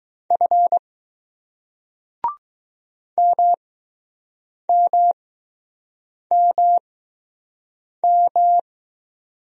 Morse code, heard as a keyed tone, about 20 wpm, spaced as for 6 wpm, 700 Hz. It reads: F MMMM